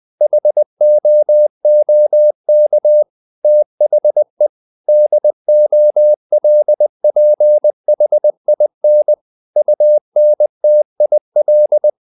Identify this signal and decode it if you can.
Morse code, heard as a keyed tone, about 20 wpm, 600 Hz. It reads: HOOK THE DOLPHIN UNTIL
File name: the old man and the sea part 2 20wpm